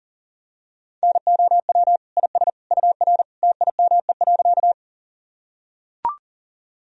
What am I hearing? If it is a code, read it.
Morse code, heard as a keyed tone, about 40 wpm, 700 Hz. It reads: NOW IS UR TIME.
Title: Sentences from Top 100 Words and Abbreviations 40wpm